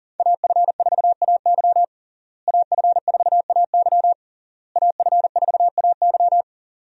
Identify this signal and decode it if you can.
Morse code, heard as a keyed tone, about 40 wpm, 700 Hz. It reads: AF4AY AF4AY AF4AY
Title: USA Call Signs - 5 Characters - Repeated 3x - 40wpm